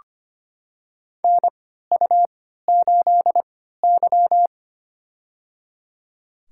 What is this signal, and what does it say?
Morse code, heard as a keyed tone, about 25 wpm, spaced as for 14 wpm, 700 Hz. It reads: NU8Y
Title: USA Call Signs - 3x Character Spacing - 25wpm